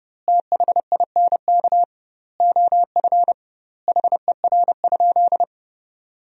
Morse code, heard as a keyed tone, about 32 wpm, 700 Hz. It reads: THINK OF HER?
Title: 2x Word-Spacing - Sentences from Top 200 Words - 30wpm